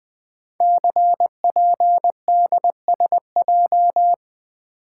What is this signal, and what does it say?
Morse code, heard as a keyed tone, about 20 wpm, 700 Hz. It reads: CPDSJ
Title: Code Groups - 5 Letter:Numbers - 20wpm